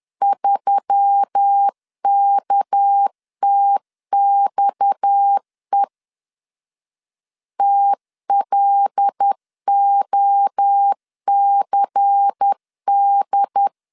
Morse code, heard as a keyed tone, about 10 wpm, 800 Hz. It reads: 3KTXE TLOCD
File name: LEZIONI / 6 PKX0 / 6 PKX 10wpm 2426